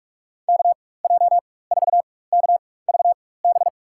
Morse code, heard as a keyed tone, about 47 wpm, spaced as for 21 wpm, 700 Hz. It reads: KJ4KVB
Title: USA Call Signs - 6 Characters - 4x Character Spacing - 45wpm